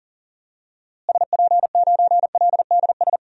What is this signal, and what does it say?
Morse code, heard as a keyed tone, about 39 wpm, 700 Hz. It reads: SP9LDS
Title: DXCC-100 Call Signs - 6 Characters - Repeated 2x - 40wpm